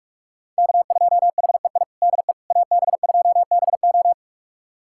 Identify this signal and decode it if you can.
Morse code, heard as a keyed tone, about 41 wpm, 700 Hz. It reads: K2HEI DE AB2BO
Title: USA Callsign DE Callsign - 5 Characters - 45wpm